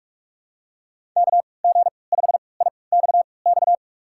Morse code, heard as a keyed tone, about 46 wpm, spaced as for 25 wpm, 700 Hz. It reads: KG5IXX
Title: USA Call Signs - 6 Characters - 3x Character Spacing - 45wpm